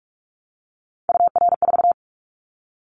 Morse code, heard as a keyed tone, about 46 wpm, 700 Hz. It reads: UR4
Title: Callsign Prefixes for Top 25 DXCC - 45wpm